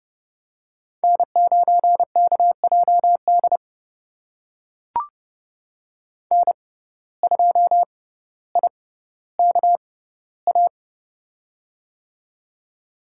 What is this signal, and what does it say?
Morse code, heard as a keyed tone, about 30 wpm, spaced as for 15 wpm, 700 Hz. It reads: N9KJD N2IKA
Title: USA Call Signs - 6x Character Spacing - 30wpm